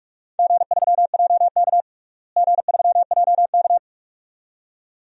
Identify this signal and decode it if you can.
Morse code, heard as a keyed tone, about 44 wpm, 700 Hz. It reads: G3JK G3JK
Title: DXCC-100 Call Signs - 4 Characters - Repeated 2x - 45wpm